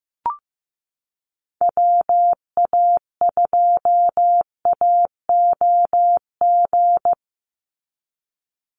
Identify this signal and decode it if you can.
Morse code, heard as a keyed tone, about 15 wpm, 700 Hz. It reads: WA2AOG
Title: Call Signs Repeated 2x - 15wpm